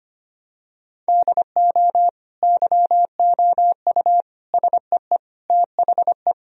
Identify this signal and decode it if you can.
Morse code, heard as a keyed tone, about 25 wpm, 700 Hz. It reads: DO YOU SEE THE
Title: Sentences from Top 1000 Words - Review Part 4 - 25wpm